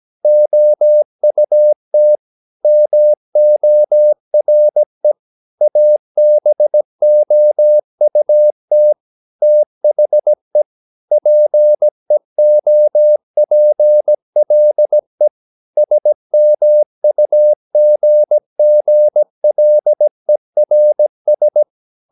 Morse code, heard as a keyed tone, about 17 wpm, 600 Hz. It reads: OUT MORE ABOUT THE PEOPLE SMUGGLERS
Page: Romanen Never af Ken Follet på engelsk.